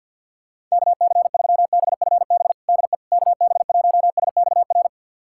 Morse code, heard as a keyed tone, about 53 wpm, 700 Hz. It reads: KC3BFB DE KB1SXR
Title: USA Callsign DE Callsign - 6 Characters - 50wpm